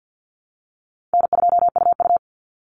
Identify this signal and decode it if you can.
Morse code, heard as a keyed tone, about 50 wpm, 700 Hz. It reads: N2UU
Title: Call Signs - 50wpm